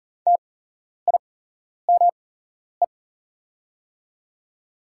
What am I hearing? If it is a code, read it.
Morse code, heard as a keyed tone, about 40 wpm, spaced as for 11 wpm, 700 Hz. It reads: TIME
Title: 8x Character Spacing - Top 100 Words - 40wpm